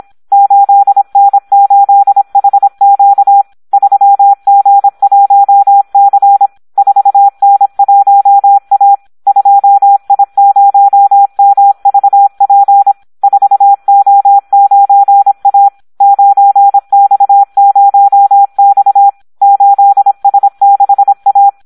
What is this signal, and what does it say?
Morse code, heard as a keyed tone, about 26 wpm, 800 Hz, with radio noise. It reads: 8N8HQ 3G1C 4N1A 2I0MVP 4O9A 9X0X 8S6A
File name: Audio practica CW / SD 1 26